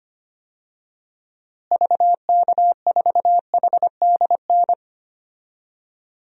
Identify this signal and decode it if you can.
Morse code, heard as a keyed tone, about 25 wpm, 700 Hz. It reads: VK4HDN
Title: DXCC-100 Call Signs - 6 Characters - Repeated 3x - 25wpm